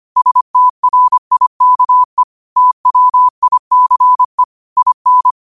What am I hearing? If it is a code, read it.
Morse code, heard as a keyed tone, about 25 wpm, 1000 Hz. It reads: STRIKE TWICE IN